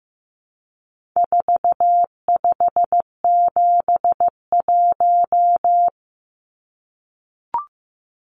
Morse code, heard as a keyed tone, about 15 wpm, 700 Hz. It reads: 4571
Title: Numbers - 4-Digit 15wpm